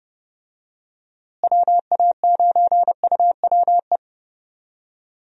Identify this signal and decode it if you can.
Morse code, heard as a keyed tone, about 30 wpm, 700 Hz. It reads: WA9UWE
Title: USA Call Signs - 6 Characters - Repeated 3x - 30wpm